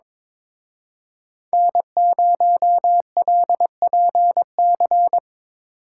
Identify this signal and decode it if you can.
Morse code, heard as a keyed tone, about 25 wpm, 700 Hz. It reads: N0LPC
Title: Speed Racing - USA 1x3 Call Signs 22wpm